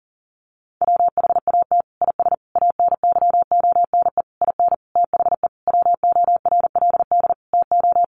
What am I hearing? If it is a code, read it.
Morse code, heard as a keyed tone, about 40 wpm, 700 Hz. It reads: WHAT IS ANYONE IN THE WORLD TO
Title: Sentences from Top 900 Words 40wpm